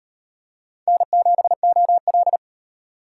Morse code, heard as a keyed tone, about 38 wpm, 700 Hz. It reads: N7OL